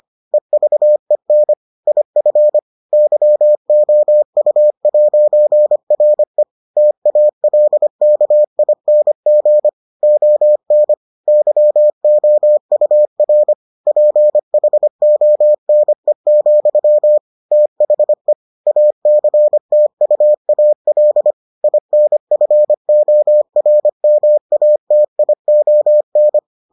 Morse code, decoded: EVEN IF YOU'RE TALKING ON YOUR PHONE, THE ACTUAL INFORMATION

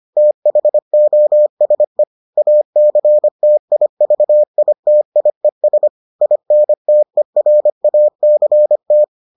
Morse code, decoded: THOSE ACTIVITIES INTERACT

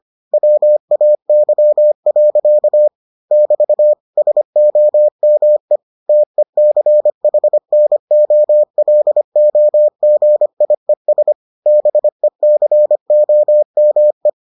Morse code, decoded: WAY. <BT> SOME TECHNOLOGIES BECOME

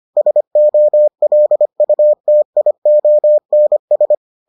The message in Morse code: SOLUTIONS

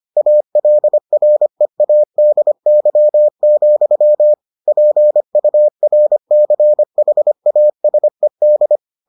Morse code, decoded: ALREADY, PURCHASED

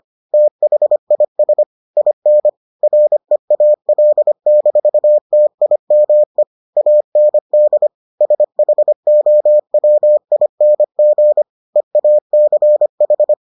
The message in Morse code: THIS IN REAL-TIME AND SHOWING EACH

